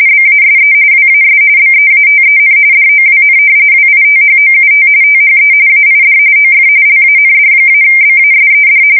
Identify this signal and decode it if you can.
RTTY, about 45.45 baud. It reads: BASE WAS BEATEN DOWN WITHOUT THE AID OF LAND-BASED
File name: fsk170 / battleshipnj